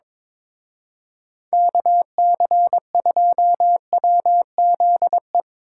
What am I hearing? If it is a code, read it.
Morse code, heard as a keyed tone, about 25 wpm, 700 Hz. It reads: KC2WZE
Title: Speed Racing - USA 2x3 Call Signs 22wpm